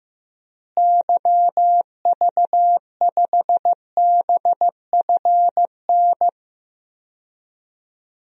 Morse code, heard as a keyed tone, about 15 wpm, 700 Hz. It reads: YV5BFN
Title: DXCC-100 Call Signs - 6 Characters - Repeated 2x - 15wpm